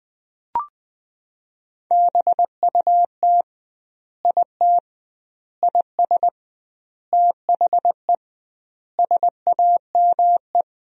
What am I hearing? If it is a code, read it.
Morse code, heard as a keyed tone, about 20 wpm, 700 Hz. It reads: BUT IT IS THE SAME
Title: 2x Word-Spacing - Sentences from Top 300 Words - 20wpm